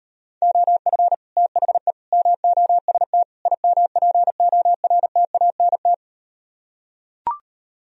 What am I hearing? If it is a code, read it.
Morse code, heard as a keyed tone, about 38 wpm, 700 Hz. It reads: OF THE MOST IMPORTANT